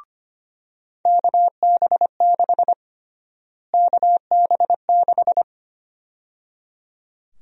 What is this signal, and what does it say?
Morse code, heard as a keyed tone, about 25 wpm, 700 Hz. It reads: KB6 KB6
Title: USA Call Sign Prefixes - Repeated 2x - 25wpm